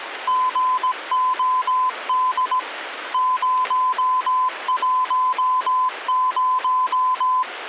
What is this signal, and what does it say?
Morse code (Russian alphabet), heard as a keyed tone, about 17 wpm, 1020 Hz. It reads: ГОД 010
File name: original